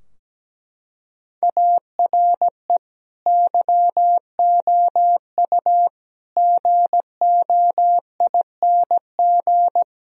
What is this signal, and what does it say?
Morse code, heard as a keyed tone, about 17 wpm, 700 Hz. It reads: ARE YOU GOING
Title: Sentences from Top 1000 Words - Review Part 2 - 17wpm